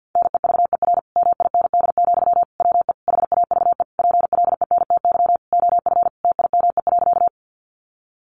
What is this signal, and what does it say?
Morse code, heard as a keyed tone, about 50 wpm, 700 Hz. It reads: NEVER MIND, WE HAVE PLENTY OF TIME.
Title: Sentences from Top 1300 Words 50wpm